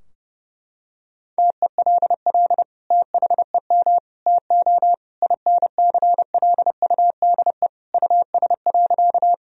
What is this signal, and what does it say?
Morse code, heard as a keyed tone, about 30 wpm, 700 Hz. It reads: TELL THEM TO INCLUDE US.
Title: Sentences from Top 1000 Words - Review Part 3 - 30wpm